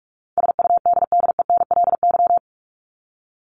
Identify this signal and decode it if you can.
Morse code, heard as a keyed tone, about 44 wpm, 700 Hz. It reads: SUDDENLY
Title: Top 1501-1750 Words 45wpm